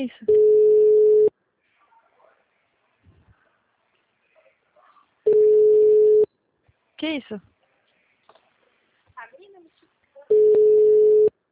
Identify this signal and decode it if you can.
Ringback tone (Europe, 425 Hz), with speech.